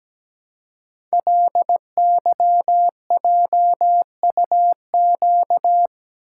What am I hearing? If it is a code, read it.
Morse code, heard as a keyed tone, about 17 wpm, 700 Hz. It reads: LYJUQ
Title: Code Groups - 5 Letters - 17wpm